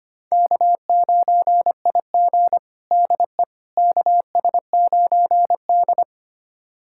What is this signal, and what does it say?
Morse code, heard as a keyed tone, about 25 wpm, 700 Hz. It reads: K9IG DE KS9D
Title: USA Callsign DE Callsign - 4 Characters - 25wpm